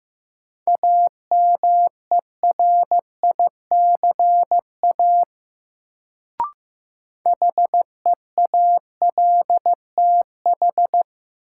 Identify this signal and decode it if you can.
Morse code, heard as a keyed tone, about 15 wpm, 700 Hz. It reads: AMERICA HEALTH
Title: Sets of 5 Words - New Word Drill - 15wpm